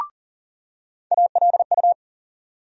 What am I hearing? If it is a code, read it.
Morse code, heard as a keyed tone, about 40 wpm, 700 Hz. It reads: ALU
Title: USA Call Sign Postfixes - 3 Letters 40wpm